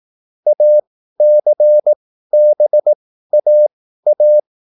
Morse code, heard as a keyed tone, about 18 wpm, spaced as for 13 wpm, 600 Hz. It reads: ACBAA
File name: QSL / CW lesson